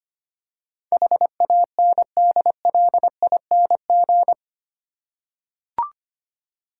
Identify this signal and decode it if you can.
Morse code, heard as a keyed tone, about 25 wpm, 700 Hz. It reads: HANDLING
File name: 8 Letter Words - First 3 Letters Given - 25wpm